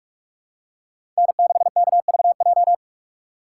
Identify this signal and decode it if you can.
Morse code, heard as a keyed tone, about 46 wpm, 700 Hz. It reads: N6KVJ
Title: USA Call Signs - 5 Characters - 3x Character Spacing - 45wpm